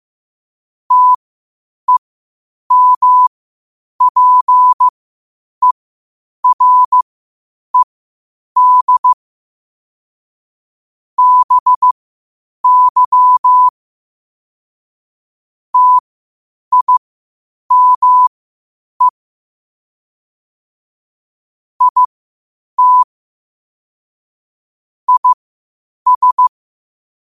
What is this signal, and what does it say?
Morse code, heard as a keyed tone, about 15 wpm, spaced as for 8 wpm, 1000 Hz. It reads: TEMPERED BY TIME IT IS
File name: qotdfiles / 8wpm